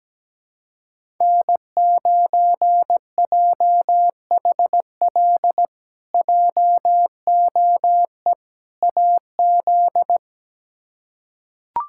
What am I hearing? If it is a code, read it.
Morse code, heard as a keyed tone, about 17 wpm, 700 Hz. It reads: N9JHL JOE AZ
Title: K1USN SST QSO Practice - 17wpm